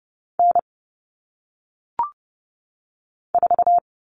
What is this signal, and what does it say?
Morse code, heard as a keyed tone, about 30 wpm, 700 Hz. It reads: N 4
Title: All Cut Numbers - 1-Digit 30wpm